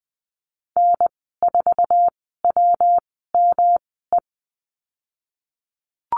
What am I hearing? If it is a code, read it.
Morse code, heard as a keyed tone, about 20 wpm, spaced as for 15 wpm, 700 Hz. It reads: N4WME
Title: 213 Lesson 34g - Number 8 - Call Signs 20wpm